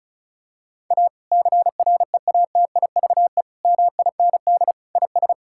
Morse code, heard as a keyed tone, about 34 wpm, 700 Hz. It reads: A CREATIVE MIND IS
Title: Sentences from Top 1000 Words - Review Part 5 - 35wpm